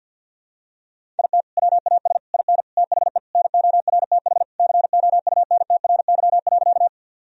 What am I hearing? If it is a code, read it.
Morse code, heard as a keyed tone, about 50 wpm, 700 Hz. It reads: IT WAS IN THE NORTH COUNTRY.
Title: Sentences from Top 1000 Words - Review Part 1 - 50wpm